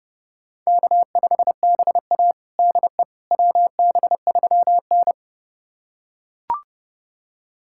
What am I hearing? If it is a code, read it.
Morse code, heard as a keyed tone, about 32 wpm, 700 Hz. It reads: K5BA DE WB3N
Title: USA Callsign DE Callsign - 4 Characters - 30wpm